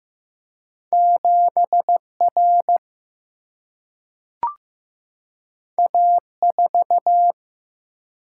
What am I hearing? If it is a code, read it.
Morse code, heard as a keyed tone, about 16 wpm, 700 Hz. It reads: 7 R A 4
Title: Single-Letter Single-Number 15wpm